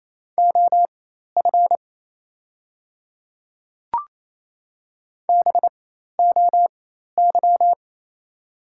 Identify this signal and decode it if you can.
Morse code, heard as a keyed tone, about 27 wpm, spaced as for 13 wpm, 690 Hz. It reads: OF BOY
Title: 4x Character Spacing - Top 200 Words - 28wpm